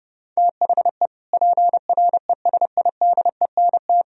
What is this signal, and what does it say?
Morse code, heard as a keyed tone, about 31 wpm, 700 Hz. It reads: THE PRESIDENT